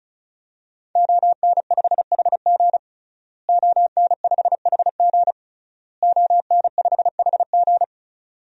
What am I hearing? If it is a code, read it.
Morse code, heard as a keyed tone, about 34 wpm, 700 Hz. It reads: ON5HG ON5HG ON5HG
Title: DXCC-100 Call Signs - 5 Characters - Repeated 3x - 35wpm